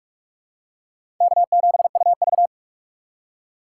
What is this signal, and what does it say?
Morse code, heard as a keyed tone, about 47 wpm, 700 Hz. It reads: K7UV
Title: USA Call Signs - 4 Characters - 6x Character Spacing - 45wpm